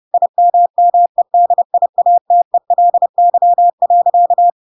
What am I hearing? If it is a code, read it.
Morse code, heard as a keyed tone, about 30 wpm, 700 Hz. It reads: IMMEDIATELY.
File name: Pinocchio / Pinocchio ch1 30 30 ik1hge